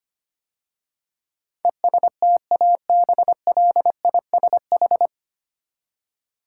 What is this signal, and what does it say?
Morse code, heard as a keyed tone, about 25 wpm, 700 Hz. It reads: ESTABLISH
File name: practice / 25wpm / 9 Letter Words - First 3 Letters Given - 25wpm